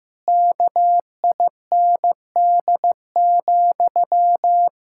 Morse code, heard as a keyed tone, about 15 wpm, 700 Hz. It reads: KIND,
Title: Pangram Challenge v2 - 15wpm